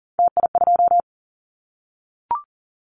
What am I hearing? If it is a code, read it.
Morse code, heard as a keyed tone, about 38 wpm, 690 Hz. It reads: TI2
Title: Callsign Prefixes for Top 100 DXCC - 40wpm